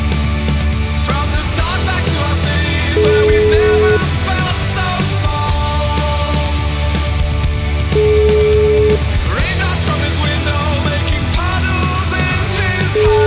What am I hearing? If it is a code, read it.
Ringback tone (Europe, 425 Hz), with noise.